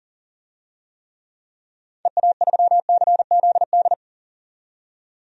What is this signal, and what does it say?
Morse code, heard as a keyed tone, about 40 wpm, 700 Hz. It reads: EA3CZD
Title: DXCC-100 Call Signs - 6 Characters - 40wpm